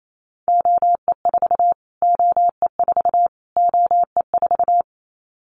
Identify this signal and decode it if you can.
Morse code, heard as a keyed tone, about 28 wpm, 700 Hz. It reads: OE4 OE4 OE4
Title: Callsign Prefixes for Top 25 DXCC - Repeated 3x - 28wpm